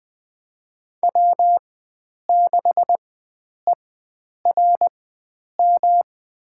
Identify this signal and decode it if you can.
Morse code, heard as a keyed tone, about 20 wpm, spaced as for 9 wpm, 700 Hz. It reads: W6ERM
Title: USA Call Signs - 5 Characters - 4x Character Spacing - 20wpm